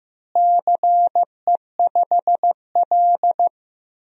Morse code, heard as a keyed tone, about 15 wpm, 700 Hz. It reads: CE5L
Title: DXCC-100 Call Signs - 4 Characters - Repeated 3x - 15wpm